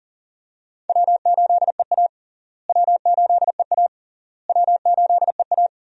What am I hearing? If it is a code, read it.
Morse code, heard as a keyed tone, about 40 wpm, 700 Hz. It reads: W8EA W8EA W8EA
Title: USA Call Signs - 4 Characters - Repeated 3x - 40wpm